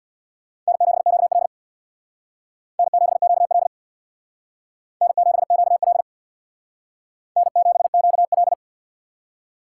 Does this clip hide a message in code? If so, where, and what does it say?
Yes: Morse code from 0.7 to 8.5 s, heard as a keyed tone, about 65 wpm, 700 Hz. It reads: N7QL N7QL N7QL N7QL